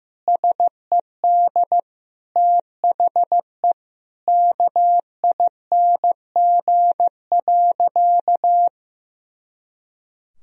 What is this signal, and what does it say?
Morse code, heard as a keyed tone, about 15 wpm, 700 Hz. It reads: SED THE KING.
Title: Sentences from Top 400 Words and Abbreviations 15wpm